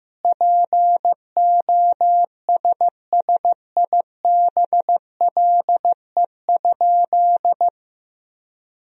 Morse code, heard as a keyed tone, about 16 wpm, 700 Hz. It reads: POSSIBLE?
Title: 2x Word-Spacing - Sentences from Top 500 Words - 15wpm